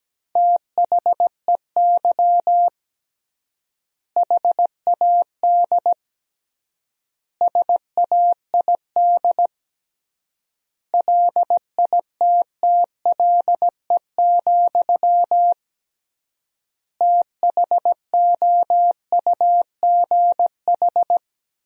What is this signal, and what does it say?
Morse code, heard as a keyed tone, about 17 wpm, 700 Hz. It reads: THEY HAD SAID LITTLE, THOUGH
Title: 3x Word-Spacing - Sentences from Top 400 Words - 17wpm